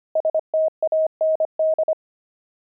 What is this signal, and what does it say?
Morse code, heard as a keyed tone, about 25 wpm, 630 Hz. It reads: STAND
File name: fs-25-cs-25